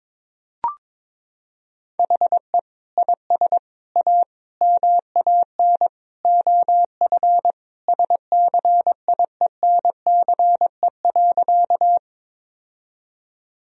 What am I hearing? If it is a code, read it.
Morse code, heard as a keyed tone, about 22 wpm, 700 Hz. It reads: HE IS A MAN OF SCIENCE.
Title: Sentences from Top 1000 Words - Review Part 2 - 22wpm